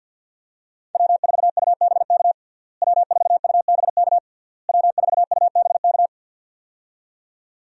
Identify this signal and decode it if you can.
Morse code, heard as a keyed tone, about 50 wpm, 700 Hz. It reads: W4UBK W4UBK W4UBK
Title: USA Call Signs - 5 Characters - Repeated 3x - 50wpm